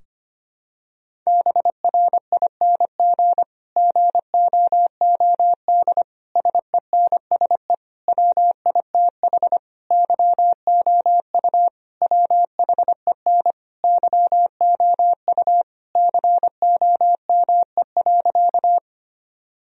Morse code, heard as a keyed tone, about 25 wpm, 700 Hz. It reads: BRING GOOD SENSE WITH YOU WHEN YOU COME.
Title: Sentences from Top 1000 Words - Review Part 4 - 25wpm